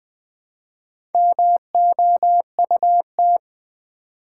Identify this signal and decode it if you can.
Morse code, heard as a keyed tone, about 20 wpm, 700 Hz. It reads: MOUT
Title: Build A Word - 1 Letter At A Time - 5-Letter Words - 20wpm